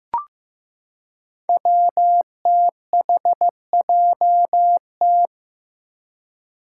Morse code, heard as a keyed tone, about 15 wpm, 700 Hz. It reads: WTHJT
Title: Five Challenging Letter-Number Combinations 15wpm